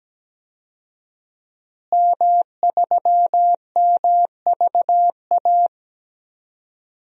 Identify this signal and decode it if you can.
Morse code, heard as a keyed tone, about 17 wpm, 700 Hz. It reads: M3MVA
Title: DXCC-100 Call Signs - 5 Characters - Repeated 2x - 17wpm